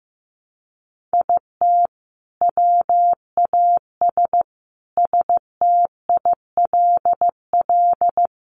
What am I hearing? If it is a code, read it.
Morse code, heard as a keyed tone, about 15 wpm, 700 Hz. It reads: IT WAS STILL
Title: Sentences from Top 1300 Words 15wpm